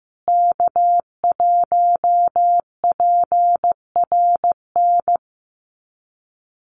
Morse code, heard as a keyed tone, about 15 wpm, 700 Hz. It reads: K1PRN
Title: USA Call Signs Repeated 2x 15wpm